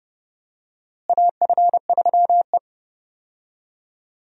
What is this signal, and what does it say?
Morse code, heard as a keyed tone, about 30 wpm, 700 Hz. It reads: AF3E
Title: USA Call Signs - 4 Characters - Repeated 3x - 30wpm